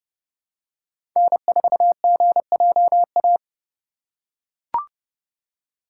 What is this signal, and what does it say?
Morse code, heard as a keyed tone, about 32 wpm, 690 Hz. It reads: N4GJA
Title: USA Call Signs - 5 Characters - 2x Character Spacing - 30wpm